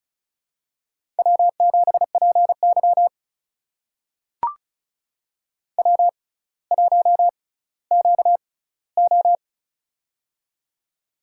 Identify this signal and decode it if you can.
Morse code, heard as a keyed tone, about 36 wpm, spaced as for 18 wpm, 700 Hz. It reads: W7PY W1QO
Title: USA Call Signs - 4 Characters - 6x Character Spacing - 35wpm